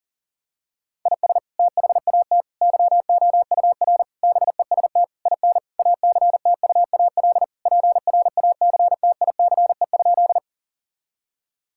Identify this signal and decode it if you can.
Morse code, heard as a keyed tone, about 37 wpm, 700 Hz. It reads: IS THAT YOUR BEST IN ACTUAL PRACTICE?
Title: Sentences from Top 1000 Words - Review Part 5 - 40wpm